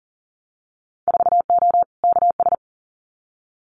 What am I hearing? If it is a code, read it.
Morse code, heard as a keyed tone, about 40 wpm, 700 Hz. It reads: 4O KS
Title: Winter Field Day Exchanges Encore 40wpm